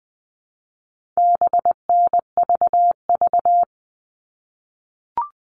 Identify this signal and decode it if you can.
Morse code, heard as a keyed tone, about 20 wpm, 700 Hz. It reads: BNVV
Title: All Cut Numbers - 4-Digit 20wpm